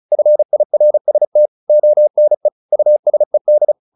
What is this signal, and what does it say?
Morse code, heard as a keyed tone, about 34 wpm, 600 Hz. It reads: FIRST ONE USED